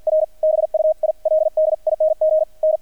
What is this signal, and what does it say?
Morse code, heard as a keyed tone, about 48 wpm, 630 Hz. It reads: R BUILDING N
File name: archived / CRSnet